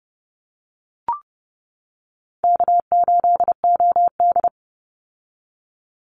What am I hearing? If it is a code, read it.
Morse code, heard as a keyed tone, about 31 wpm, 700 Hz. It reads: K8OD
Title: Call Signs - 30wpm